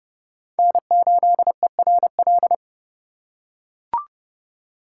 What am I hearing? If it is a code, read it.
Morse code, heard as a keyed tone, about 30 wpm, 700 Hz. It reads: N8ERL
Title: USA Call Signs - 5 Characters - 3x Character Spacing - 30wpm